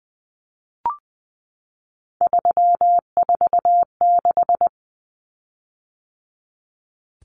Morse code, heard as a keyed tone, about 20 wpm, 700 Hz. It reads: 346
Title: Common Cut Numbers - 3-Digit 20wpm